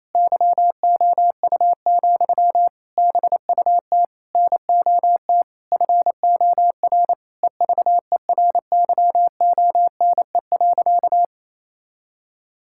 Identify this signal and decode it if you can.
Morse code, heard as a keyed tone, about 28 wpm, 700 Hz. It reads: YOU, BUT NOT FOR EVERYONE.
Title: Sentences from Top 1000 Words - Review Part 5 - 28wpm